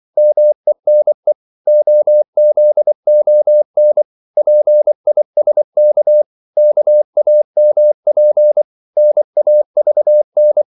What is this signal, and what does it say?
Morse code, heard as a keyed tone, about 24 wpm, 600 Hz. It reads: MENE OZON PISK KAMP NAVN
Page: Korte danske ord 24wpm